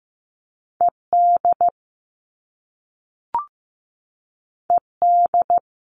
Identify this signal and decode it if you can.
Morse code, heard as a keyed tone, about 15 wpm, 700 Hz. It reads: ED ED
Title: All Cut Numbers - 2-Digit 15wpm